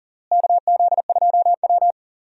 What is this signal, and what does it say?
Morse code, heard as a keyed tone, about 40 wpm, 700 Hz. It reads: KZ2W
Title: USA Call Signs - 4 Characters - Repeated 3x - 40wpm